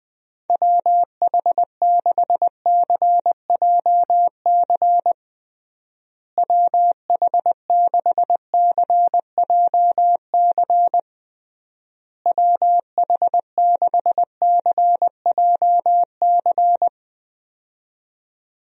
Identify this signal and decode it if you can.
Morse code, heard as a keyed tone, about 21 wpm, 700 Hz. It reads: WH6CJC WH6CJC WH6CJC
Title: USA Call Signs - 6 Characters - Repeated 3x - 20wpm